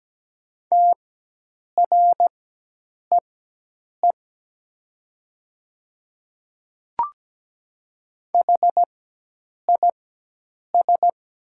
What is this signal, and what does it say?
Morse code, heard as a keyed tone, about 17 wpm, spaced as for 8 wpm, 700 Hz. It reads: TREE HIS
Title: 4x Character Spacing - Top 300 Words - 17wpm